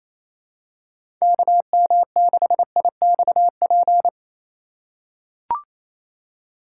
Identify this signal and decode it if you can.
Morse code, heard as a keyed tone, about 28 wpm, 700 Hz. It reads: KM6IXP
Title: USA Call Signs - 3x Character Spacing - 28wpm